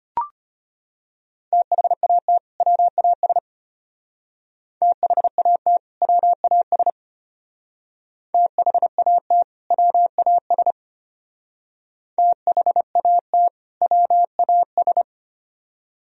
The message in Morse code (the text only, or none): THAT WAS THAT WAS THAT WAS THAT WAS